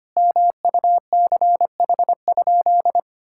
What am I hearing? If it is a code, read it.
Morse code, heard as a keyed tone, about 25 wpm, 700 Hz. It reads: MUCH?
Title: Sentences from Top 1000 Words - Review Part 5 - 25wpm